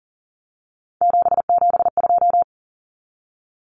Morse code, heard as a keyed tone, about 40 wpm, 700 Hz. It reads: 772
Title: Numbers - 3-Digit 40wpm